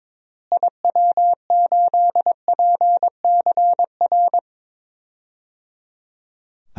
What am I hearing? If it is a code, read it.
Morse code, heard as a keyed tone, about 22 wpm, 700 Hz. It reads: IW8PCR
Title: DXCC-100 Call Signs - 6 Characters - Repeated 3x - 22wpm